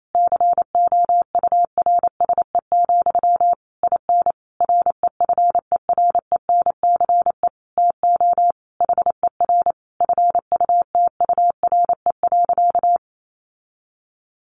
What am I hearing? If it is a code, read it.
Morse code, heard as a keyed tone, about 28 wpm, 700 Hz. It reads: COURSE, IN REFERENCE TO HER FUTURE.
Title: Sentences from Top 1200 Words 28wpm